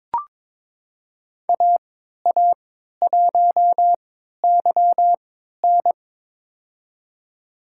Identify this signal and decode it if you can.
Morse code, heard as a keyed tone, about 22 wpm, spaced as for 13 wpm, 700 Hz. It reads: AA1YN
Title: USA Call Signs - 5 Characters - 3x Character Spacing - 22wpm